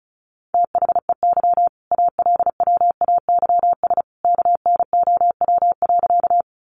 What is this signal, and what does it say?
Morse code, heard as a keyed tone, about 34 wpm, 700 Hz. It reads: THEY ALWAYS KNOW.
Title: Sentences from Top 300 Words 35wpm